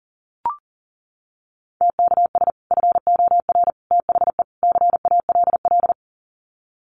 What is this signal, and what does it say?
Morse code, heard as a keyed tone, about 39 wpm, 700 Hz. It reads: TKS FOR THE CALL
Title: Common QSO Phrases 40wpm